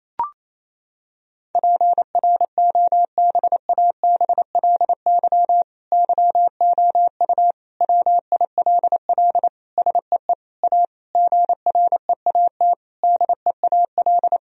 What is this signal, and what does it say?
Morse code, heard as a keyed tone, about 28 wpm, 700 Hz. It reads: PROBABLY YOU WILL SEE A GREAT DEAL
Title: Sentences from Top 1000 Words - Review Part 4 - 28wpm